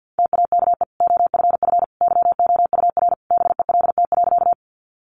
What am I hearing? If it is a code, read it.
Morse code, heard as a keyed tone, about 50 wpm, 700 Hz. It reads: TAKE OFF YOUR BELT.
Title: Sentences from Top 1400 Words 50wpm